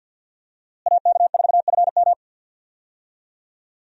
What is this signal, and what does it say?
Morse code, heard as a keyed tone, about 52 wpm, 700 Hz. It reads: AK4FM
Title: USA Call Signs - 5 Characters - Repeated 2x - 50wpm